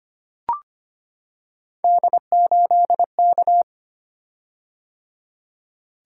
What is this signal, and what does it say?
Morse code, heard as a keyed tone, about 25 wpm, 700 Hz. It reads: D8K
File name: Three Letter-Number Combinations - 25wpm